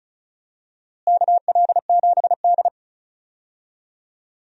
Morse code, heard as a keyed tone, about 36 wpm, 700 Hz. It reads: KL7D
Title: DXCC-100 Call Signs - 4 Characters - Repeated 3x - 35wpm